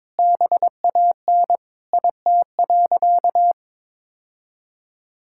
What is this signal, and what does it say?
Morse code, heard as a keyed tone, about 22 wpm, 700 Hz. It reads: BAN IT.
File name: Sentences with 7-Word Phrases - 22wpm